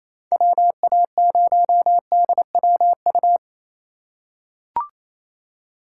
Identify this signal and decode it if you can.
Morse code, heard as a keyed tone, about 28 wpm, 700 Hz. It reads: WA0DWU
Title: USA Call Signs - 6 Characters - 3x Character Spacing - 28wpm